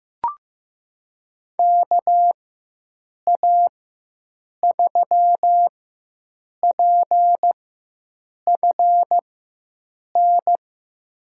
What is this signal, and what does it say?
Morse code, heard as a keyed tone, about 15 wpm, spaced as for 7 wpm, 700 Hz. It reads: KA3PFN
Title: USA Call Signs - 6 Characters - 4x Character Spacing - 15wpm